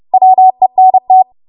Morse code, heard as a keyed tone, about 31 wpm, 750 Hz. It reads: WENT
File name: Morse / 30WPM